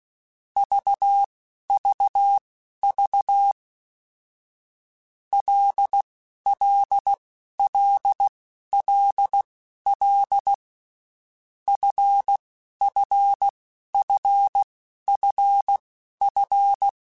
Morse code, heard as a keyed tone, about 16 wpm, spaced as for 11 wpm, 800 Hz. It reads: VVV LLLLL FFFFF
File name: Groups@10 / Groups@10 mp3s / LFCYQ@10